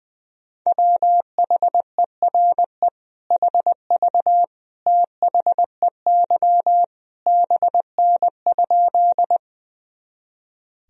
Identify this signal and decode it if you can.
Morse code, heard as a keyed tone, about 20 wpm, 700 Hz. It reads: WHERE HV THEY BN?
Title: Sentences from Top 200 Words and Abbreviations 20wpm